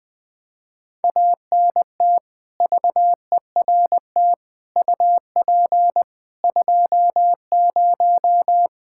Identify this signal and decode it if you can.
Morse code, heard as a keyed tone, about 20 wpm, 700 Hz. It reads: ANT VERT UP 20
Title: Practice QSO - Exchange 2 - Info - 20wpm